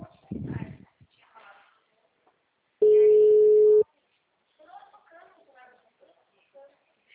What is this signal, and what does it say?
Ringback tone (Europe, 425 Hz), with speech.